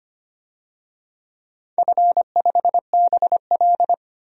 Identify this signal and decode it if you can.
Morse code, heard as a keyed tone, about 25 wpm, 700 Hz. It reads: F5BL
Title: DXCC-100 Call Signs - 4 Characters - 25wpm